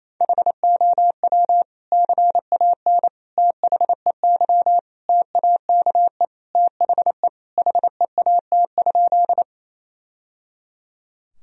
Morse code, heard as a keyed tone, about 28 wpm, 700 Hz. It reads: HOW CAN THEY TAKE THE HEAT?
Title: Sentences from Top 1000 Words - Review Part 3 - 28wpm